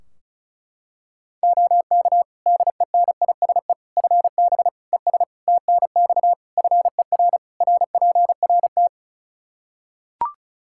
Morse code, heard as a keyed tone, about 36 wpm, 700 Hz. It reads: OK DENISE FB ES TNX FER RPRT
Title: K1USN SST QSO Practice 35wpm